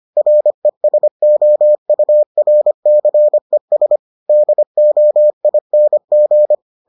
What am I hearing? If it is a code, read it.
Morse code, heard as a keyed tone, about 25 wpm, 600 Hz. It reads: RESOURCES DOING